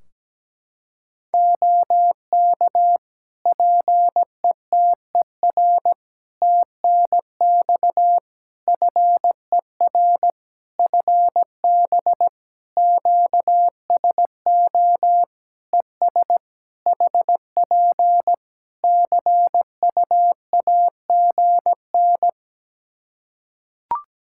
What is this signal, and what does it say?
Morse code, heard as a keyed tone, about 17 wpm, 700 Hz. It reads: OK PETER TNX FER FB QSO ES HP CUAGN
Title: Practice QSO - Exchange 4 - Ending - 17wpm